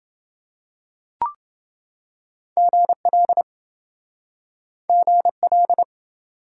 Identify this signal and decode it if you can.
Morse code, heard as a keyed tone, about 29 wpm, 700 Hz. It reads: GL GL